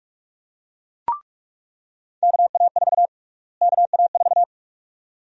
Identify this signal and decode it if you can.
Morse code, heard as a keyed tone, about 46 wpm, 700 Hz. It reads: KA4 KA4
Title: USA Call Sign Prefixes - Repeated 2x - 45wpm